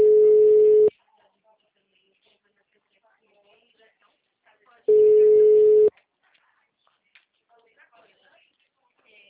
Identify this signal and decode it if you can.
Ringback tone (Europe, 425 Hz), with speech.
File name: out- -2028- - - .20221